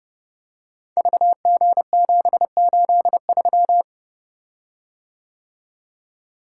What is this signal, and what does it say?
Morse code, heard as a keyed tone, about 32 wpm, 700 Hz. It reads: VG783
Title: Code Groups - 5 Letter:Numbers - 30wpm